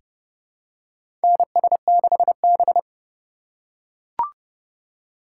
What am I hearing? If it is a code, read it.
Morse code, heard as a keyed tone, about 30 wpm, 690 Hz. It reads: NS6B
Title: USA Call Signs - 4 Characters - 2x Character Spacing - 30wpm